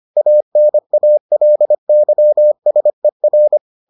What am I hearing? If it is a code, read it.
Morse code, heard as a keyed tone, about 25 wpm, 600 Hz. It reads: ANALYSER